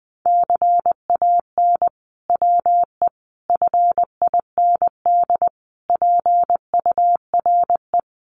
Morse code, heard as a keyed tone, about 19 wpm, 700 Hz. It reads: CAN WE FIND PURE
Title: Sentences from Top 1500 Words 20wpm